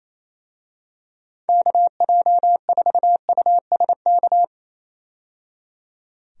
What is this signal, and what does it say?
Morse code, heard as a keyed tone, about 27 wpm, 700 Hz. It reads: KJ4USK
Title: USA Call Signs - 6 Characters - 28wpm